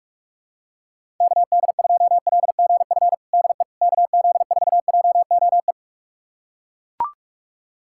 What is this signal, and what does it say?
Morse code, heard as a keyed tone, about 42 wpm, 700 Hz. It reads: KD2LGF DE KZ4JOE
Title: USA Callsign DE Callsign - 6 Characters - 45wpm